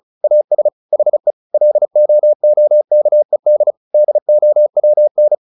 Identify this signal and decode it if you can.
Morse code, heard as a keyed tone, about 36 wpm, 600 Hz. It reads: AS HE LOOKED DOWN
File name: the old man and the sea / the old man and the sea 1 35wpm